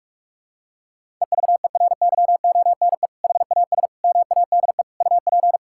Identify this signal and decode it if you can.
Morse code, heard as a keyed tone, about 47 wpm, 700 Hz. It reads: EVERYONE HAS MADE UP
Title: Sentences from Top 1000 Words - Review Part 5 - 45wpm